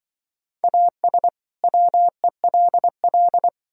Morse code, heard as a keyed tone, about 25 wpm, 700 Hz. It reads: AS WELL
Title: Speed-Racing - Sets of 2 Words 20wpm